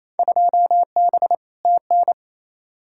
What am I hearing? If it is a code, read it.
Morse code, heard as a keyed tone, about 28 wpm, 700 Hz. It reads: 2B TN
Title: Field Day Exchanges Encore - 28wpm